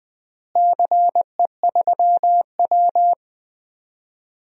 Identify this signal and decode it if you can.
Morse code, heard as a keyed tone, about 20 wpm, 700 Hz. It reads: CE3W
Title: DXCC-100 Call Signs - 4 Characters - Repeated 3x - 20wpm